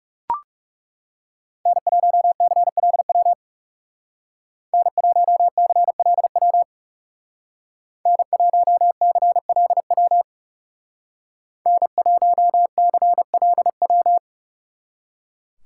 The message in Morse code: N1CLW N1CLW N1CLW N1CLW

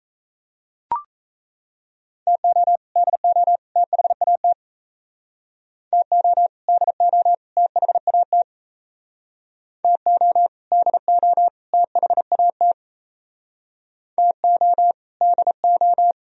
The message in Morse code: TO DO THAT TO DO THAT TO DO THAT TO DO

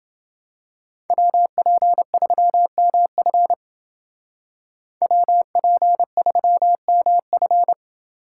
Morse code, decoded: WP3MF WP3MF

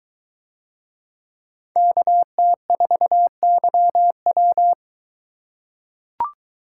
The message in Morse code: KT4YW

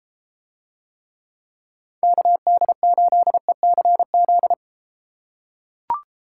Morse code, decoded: KD8ECZ

